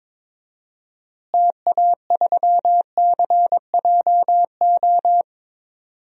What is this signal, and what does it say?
Morse code, heard as a keyed tone, about 22 wpm, 700 Hz. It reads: TA3CJO
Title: DXCC-100 Call Signs - 6 Characters - Repeated 3x - 22wpm